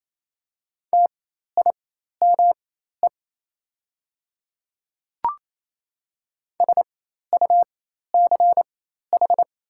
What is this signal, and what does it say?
Morse code, heard as a keyed tone, about 28 wpm, spaced as for 13 wpm, 700 Hz. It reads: TIME SUCH
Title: 4x Character Spacing - Top 200 Words - 28wpm